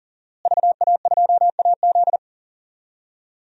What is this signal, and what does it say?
Morse code, heard as a keyed tone, about 37 wpm, 700 Hz. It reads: VA2AZ
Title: DXCC-100 Call Signs - 5 Characters - Repeated 3x - 40wpm